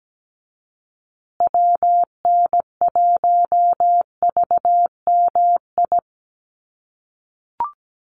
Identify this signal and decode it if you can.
Morse code, heard as a keyed tone, about 17 wpm, 700 Hz. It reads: WN1VMI
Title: USA Call Signs Repeated 3x 17wpm